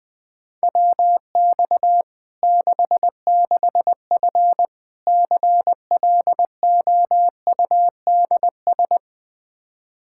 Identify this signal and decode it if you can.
Morse code, heard as a keyed tone, about 20 wpm, 700 Hz. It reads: WX 66F CLOUDS
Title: Practice QSO - Exchange 2 - Info - 20wpm